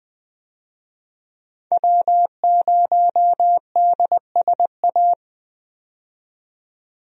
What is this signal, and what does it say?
Morse code, heard as a keyed tone, about 20 wpm, 700 Hz. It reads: W0DSA
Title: USA Call Signs - 5 Characters - 20wpm